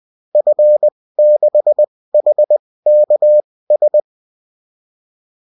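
Morse code, heard as a keyed tone, about 20 wpm, spaced as for 16 wpm, 600 Hz. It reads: F6HKS
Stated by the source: Morse code bulletin of a French amateur CW union, sent as text and in CW.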